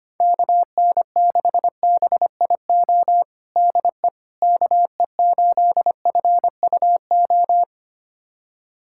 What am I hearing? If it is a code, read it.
Morse code, heard as a keyed tone, about 25 wpm, 700 Hz. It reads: KN6BIO DE KE8FUO
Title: USA Callsign DE Callsign - 6 Characters - 25wpm